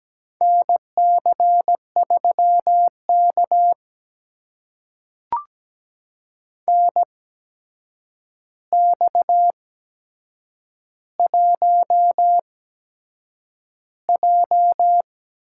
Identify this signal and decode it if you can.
Morse code, heard as a keyed tone, about 17 wpm, spaced as for 7 wpm, 700 Hz. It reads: NC3K NX1J
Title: USA Call Signs - 4 Characters - 8x Character Spacing - 17wpm